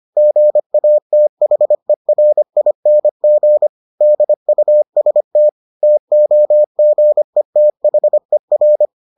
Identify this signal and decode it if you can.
Morse code, heard as a keyed tone, about 25 wpm, 600 Hz. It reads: GATHERING DUST TOGETHER